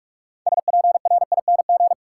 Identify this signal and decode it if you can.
Morse code, heard as a keyed tone, about 46 wpm, 700 Hz. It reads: SPRING
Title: Sentences from Top 1200 Words - New Word Drill Alt - 45wpm